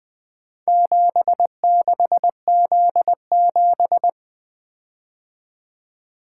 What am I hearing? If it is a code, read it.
Morse code, heard as a keyed tone, about 24 wpm, 700 Hz. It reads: 76Z7
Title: Speed-Racing - Four Letter-Number Combinations - 20wpm